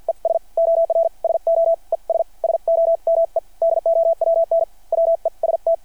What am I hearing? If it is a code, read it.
Morse code, heard as a keyed tone, about 51 wpm, spaced as for 43 wpm, 660 Hz. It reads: ES OR SO ES SOME DOWN WEST
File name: archived / CRSnet